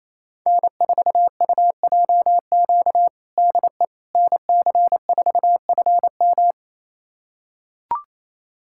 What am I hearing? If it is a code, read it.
Morse code, heard as a keyed tone, about 28 wpm, 700 Hz. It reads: N4UJQ DE NC4FM
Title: USA Callsign DE Callsign - 5 Characters - 28wpm